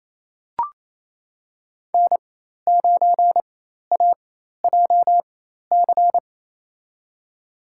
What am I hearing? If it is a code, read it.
Morse code, heard as a keyed tone, about 28 wpm, spaced as for 13 wpm, 690 Hz. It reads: N9AJC
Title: USA Call Signs - 5 Characters - 4x Character Spacing - 28wpm